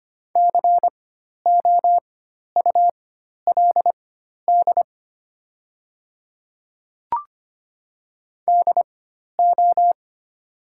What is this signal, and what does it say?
Morse code, heard as a keyed tone, about 25 wpm, spaced as for 12 wpm, 700 Hz. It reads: COULD DO
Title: 4x Character Spacing - Top 300 Words - 25wpm